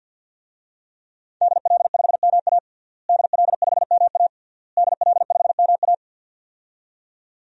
Morse code, heard as a keyed tone, about 51 wpm, 700 Hz. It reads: DL5MA DL5MA DL5MA
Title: DXCC-100 Call Signs - 5 Characters - Repeated 3x - 50wpm